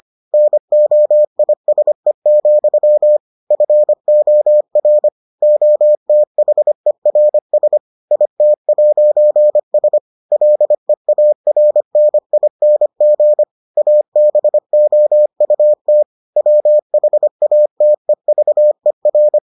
Morse code, heard as a keyed tone, about 25 wpm, 600 Hz. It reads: NOISE, FOR OTHERS IT'S LEARNING ABOUT WHATEVER